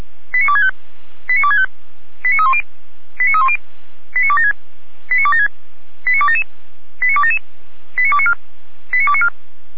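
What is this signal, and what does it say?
Five-tone selective call (ZVEI), heard as quick runs of five tones, tones of about 70 ms. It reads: 89265 89265 89310 89310 89276 89276 89277 89277 89294 89294
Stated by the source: ZVEI five-tone selective call received via SDR.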